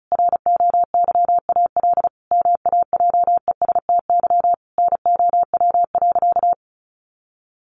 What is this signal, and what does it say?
Morse code, heard as a keyed tone, about 34 wpm, 700 Hz. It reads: ROYAL MAJESTY NOW.
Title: A Sound Of Thunder 35wpm